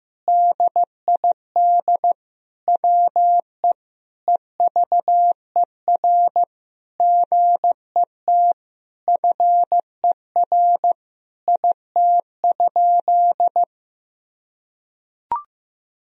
Morse code, heard as a keyed tone, about 15 wpm, 700 Hz. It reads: DID WE EVER GET FER IT?
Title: Sentences from Top 400 Words and Abbreviations 15wpm